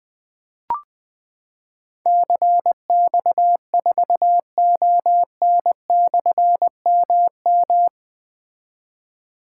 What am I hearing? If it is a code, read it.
Morse code, heard as a keyed tone, about 20 wpm, 700 Hz. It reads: CX4ON/MM
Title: Difficult Call Signs 20wpm